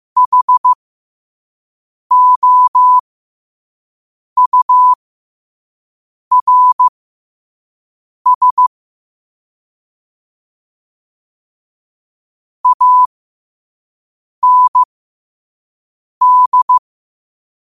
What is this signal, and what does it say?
Morse code, heard as a keyed tone, about 15 wpm, spaced as for 5 wpm, 1000 Hz. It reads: HOURS AND
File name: qotdfiles / 5wpm